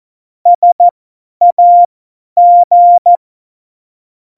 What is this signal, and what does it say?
Morse code, heard as a keyed tone, about 14 wpm, spaced as for 10 wpm, 700 Hz. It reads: SAG
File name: Morsekurs / Witze / Witz 18. Februar 2020, 50 F70